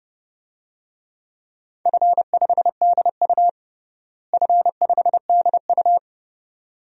Morse code, heard as a keyed tone, about 32 wpm, 700 Hz. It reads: F5DU F5DU
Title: DXCC-100 Call Signs - 4 Characters - Repeated 2x - 30wpm